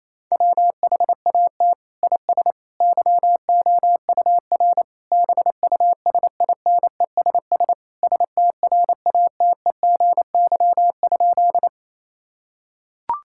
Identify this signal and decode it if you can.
Morse code, heard as a keyed tone, about 28 wpm, 700 Hz. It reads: WHAT IS YOUR BUSINESS STRATEGY?
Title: Sentences from Top 1000 Words - Review Part 5 - 28wpm